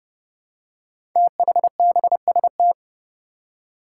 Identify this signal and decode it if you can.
Morse code, heard as a keyed tone, about 29 wpm, 700 Hz. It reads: THBST
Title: Code Groups - 5 Letters - 30wpm